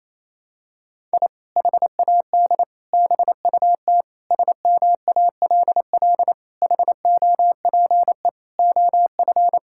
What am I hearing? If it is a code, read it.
Morse code, heard as a keyed tone, about 28 wpm, 700 Hz. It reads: I HAD BUT SMALL HOPE OF
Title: Sentences from Top 1000 Words - Review Part 6 - 28wpm